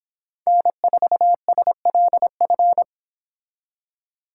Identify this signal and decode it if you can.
Morse code, heard as a keyed tone, about 25 wpm, 700 Hz. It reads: N4SLF